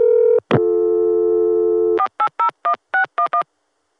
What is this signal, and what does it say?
DTMF, heard as telephone touch tones, tones of about 90 ms. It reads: 4801611